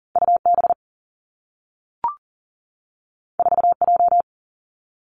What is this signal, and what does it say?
Morse code, heard as a keyed tone, about 42 wpm, 700 Hz. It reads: U B 4 J
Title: Two Challenging Letter-Number Combinations 40wpm